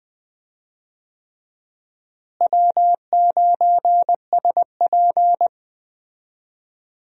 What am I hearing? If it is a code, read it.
Morse code, heard as a keyed tone, about 20 wpm, 700 Hz. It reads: W9SP
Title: USA Call Signs - 4 Characters - Repeated 3x - 20wpm